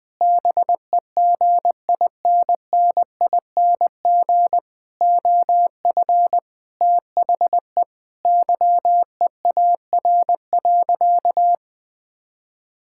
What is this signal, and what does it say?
Morse code, heard as a keyed tone, about 20 wpm, 700 Hz. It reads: BEGINNING OF THE YEAR.
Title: Sentences with 6-Word Phrases - Spoken Beginning - 20wpm